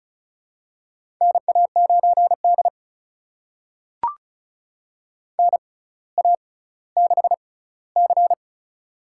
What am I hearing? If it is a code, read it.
Morse code, heard as a keyed tone, about 34 wpm, spaced as for 18 wpm, 700 Hz. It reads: NA9D NA6C
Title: USA Call Signs - 4 Characters - 6x Character Spacing - 35wpm